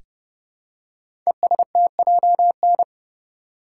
Morse code, heard as a keyed tone, about 30 wpm, 700 Hz. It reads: ESTJN
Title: Code Groups - 5 Letter:Numbers - 30wpm